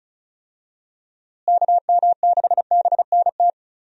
Morse code, heard as a keyed tone, about 36 wpm, 700 Hz. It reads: KM6BNT
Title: USA Call Signs - 6 Characters - Repeated 2x - 35wpm